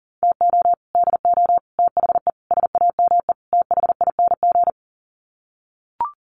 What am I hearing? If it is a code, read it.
Morse code, heard as a keyed tone, about 40 wpm, 700 Hz. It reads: TO DO THE SAME THING
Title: Sets of 5 Words Encore 40wpm